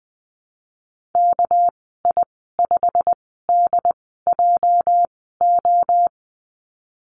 Morse code, heard as a keyed tone, about 20 wpm, 700 Hz. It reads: K I 5 D J O
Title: 238 Lesson 38g - Letter q - Call Signs 20wpm